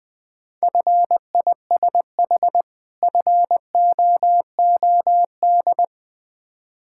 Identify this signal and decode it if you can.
Morse code, heard as a keyed tone, about 20 wpm, 700 Hz. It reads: FISH FOOD
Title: Difficult Words with Hints - 20wpm